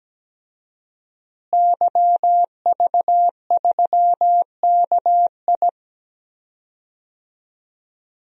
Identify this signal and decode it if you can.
Morse code, heard as a keyed tone, about 17 wpm, 700 Hz. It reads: YV3KI
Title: DXCC-100 Call Signs - 5 Characters - Repeated 2x - 17wpm